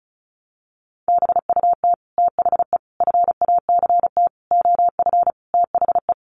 Morse code, heard as a keyed tone, about 34 wpm, 700 Hz. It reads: BUT THE FACT OF THE
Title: Sets of 5 Words Encore2 35wpm